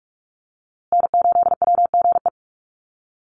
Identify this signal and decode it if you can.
Morse code, heard as a keyed tone, about 44 wpm, 690 Hz. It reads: N8WGE
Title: USA Call Signs Repeated 3x 45wpm